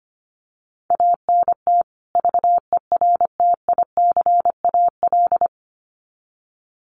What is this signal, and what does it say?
Morse code, heard as a keyed tone, about 25 wpm, 700 Hz. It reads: ANT VERTICAL
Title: Common QSO Phrases 25wpm